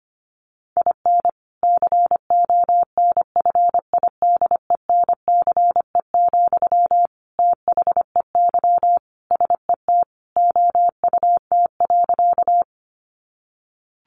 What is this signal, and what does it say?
Morse code, heard as a keyed tone, about 25 wpm, 700 Hz. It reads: IN CONFIDENCE, THEY SET OUT.
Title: Sentences from Top 1500 Words 25wpm